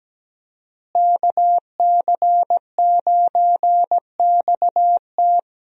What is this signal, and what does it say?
Morse code, heard as a keyed tone, about 17 wpm, 700 Hz. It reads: KC9XT
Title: USA Call Signs - 5 Characters - Repeated 2x - 17wpm